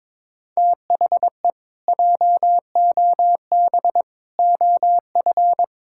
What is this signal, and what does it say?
Morse code, heard as a keyed tone, about 22 wpm, 700 Hz. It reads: THE JOB OF
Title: Pangram Challenge v2 - 22wpm